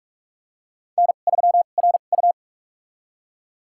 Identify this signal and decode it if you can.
Morse code, heard as a keyed tone, about 46 wpm, spaced as for 32 wpm, 700 Hz. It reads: N3RU
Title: USA Call Signs - 4 Characters - 2x Character Spacing - 45wpm